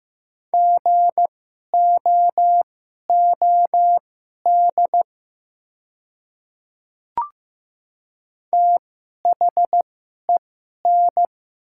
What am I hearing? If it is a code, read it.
Morse code, heard as a keyed tone, about 15 wpm, spaced as for 11 wpm, 700 Hz. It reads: GOOD THEN
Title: 2x Character Spacing - Top 300 Words - 15wpm